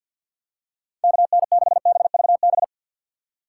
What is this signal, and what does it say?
Morse code, heard as a keyed tone, about 52 wpm, 690 Hz. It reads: KN6BVB
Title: USA Call Signs - 4x Character Spacing - 50wpm